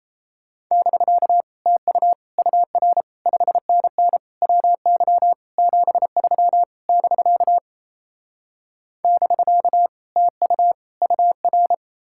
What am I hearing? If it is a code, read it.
Morse code, heard as a keyed tone, about 38 wpm, 700 Hz. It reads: <BK> TU UR 5NN WY 73 <BK> <BK> TU UR